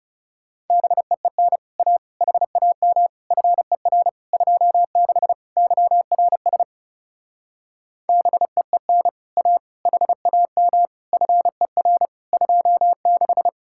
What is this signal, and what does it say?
Morse code, heard as a keyed tone, about 37 wpm, 700 Hz. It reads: BEEN A HAM FER 26 YRS BEEN A HAM FER 26
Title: Speed-Racing Common QSO Phrases Encore - 30wpm